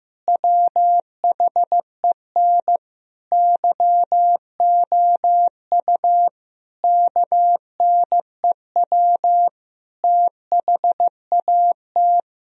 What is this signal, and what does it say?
Morse code, heard as a keyed tone, about 16 wpm, 700 Hz. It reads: WHEN YOU KNEW THAT
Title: Sentences from Top 1000 Words - Review Part 2 - 15wpm